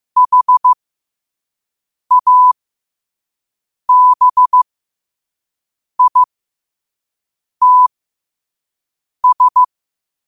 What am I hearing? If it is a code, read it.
Morse code, heard as a keyed tone, about 15 wpm, spaced as for 5 wpm, 1000 Hz. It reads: HABITS
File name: qotdfiles / 5wpm